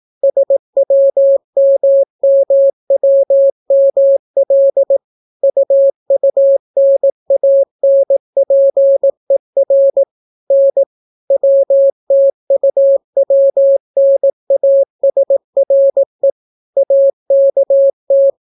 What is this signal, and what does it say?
Morse code, heard as a keyed tone, about 18 wpm, 550 Hz. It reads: SWMMWML UUNANPER N WTUWNASRE AKT